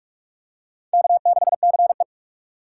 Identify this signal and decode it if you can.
Morse code, heard as a keyed tone, about 44 wpm, 700 Hz. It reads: K6CE